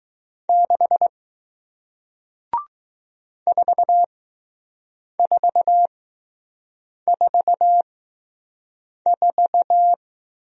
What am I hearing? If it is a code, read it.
Morse code, heard as a keyed tone, about 20 wpm, spaced as for 6 wpm, 700 Hz. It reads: 6 4444